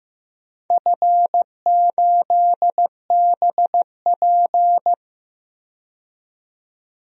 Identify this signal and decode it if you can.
Morse code, heard as a keyed tone, about 15 wpm, 700 Hz. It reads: F8BP
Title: DXCC-100 Call Signs - 4 Characters - 15wpm